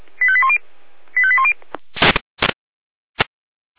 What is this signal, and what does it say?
Five-tone selective call (ZVEI), heard as quick runs of five tones, tones of about 70 ms. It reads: 85810 85810